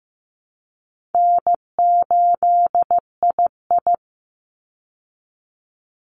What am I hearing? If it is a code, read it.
Morse code, heard as a keyed tone, about 15 wpm, 700 Hz. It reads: N8II
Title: Easy Call Signs Repeated 2x - 15wpm